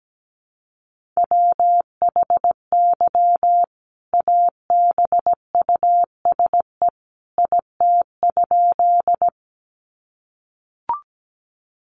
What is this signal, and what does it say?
Morse code, heard as a keyed tone, about 17 wpm, 700 Hz. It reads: WHY ABUSE IT?
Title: Sentences from Top 1300 Words 17wpm